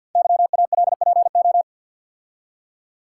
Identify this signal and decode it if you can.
Morse code, heard as a keyed tone, about 50 wpm, 700 Hz. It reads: YALPO
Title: Code Groups - 5 Letters - 50wpm